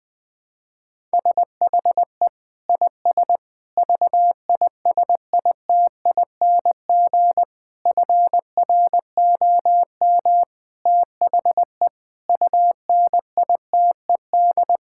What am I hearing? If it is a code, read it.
Morse code, heard as a keyed tone, about 20 wpm, 700 Hz. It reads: SHE IS VISITING FROM THE UNITED
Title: Sentences with 4-Word Phrases - Spoken Beginning - 20wpm